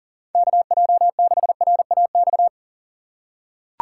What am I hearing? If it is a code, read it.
Morse code, heard as a keyed tone, about 39 wpm, 690 Hz. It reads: KJ6RAX
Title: USA Call Signs - 4x Character Spacing - 40wpm